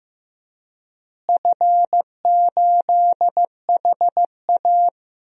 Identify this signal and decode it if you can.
Morse code, heard as a keyed tone, about 15 wpm, 700 Hz. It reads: F8HA
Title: DXCC-100 Call Signs - 4 Characters - Repeated 2x - 15wpm